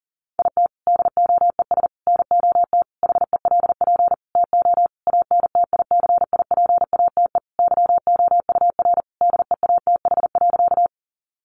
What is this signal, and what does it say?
Morse code, heard as a keyed tone, about 40 wpm, 700 Hz. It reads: IT DOES NOT HELP TO ANTICIPATE YOUR DEATH.
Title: Sentences from Top 1500 Words 40wpm